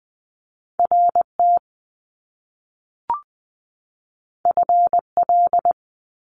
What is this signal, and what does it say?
Morse code, heard as a keyed tone, about 20 wpm, 700 Hz. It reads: RT FL
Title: Top Two Letter Combinations in Words 20wpm